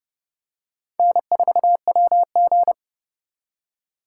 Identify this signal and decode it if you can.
Morse code, heard as a keyed tone, about 30 wpm, 700 Hz. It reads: N4WG
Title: USA Call Signs - 3x Character Spacing - 30wpm